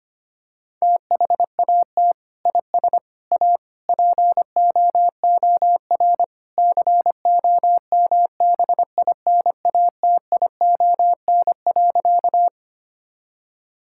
Morse code, heard as a keyed tone, about 25 wpm, 700 Hz. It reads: THAT IS A POOR COMBINATION.
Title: Sentences from Top 1000 Words - Review Part 6 - 25wpm